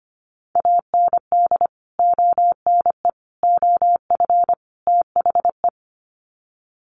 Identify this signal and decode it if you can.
Morse code, heard as a keyed tone, about 25 wpm, 700 Hz. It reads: AND ONE OF THE
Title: Sets of 4 Words 25wpm